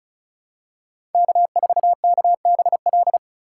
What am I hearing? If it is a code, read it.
Morse code, heard as a keyed tone, about 36 wpm, 700 Hz. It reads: K4KBL
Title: K1USN SST QSO Practice - 35wpm